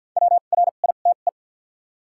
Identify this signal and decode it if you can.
Morse code, heard as a keyed tone, about 47 wpm, spaced as for 35 wpm, 700 Hz. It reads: WRITE
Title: 2x Character Spacing - Top 300 Words - 50wpm